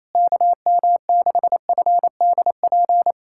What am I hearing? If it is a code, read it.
Morse code, heard as a keyed tone, about 28 wpm, 700 Hz. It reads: KM6FDP
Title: USA Callsign DE Callsign - 6 Characters - 28wpm